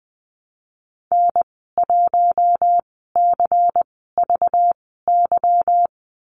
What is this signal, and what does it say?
Morse code, heard as a keyed tone, about 20 wpm, spaced as for 15 wpm, 700 Hz. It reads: N1CVY
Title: 213 Lesson 34g - Number 8 - Call Signs 20wpm